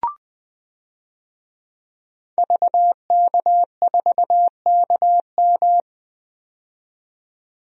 Morse code, heard as a keyed tone, about 20 wpm, 700 Hz. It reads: VK4KM
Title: DXCC-100 Call Signs - 5 Characters - 20wpm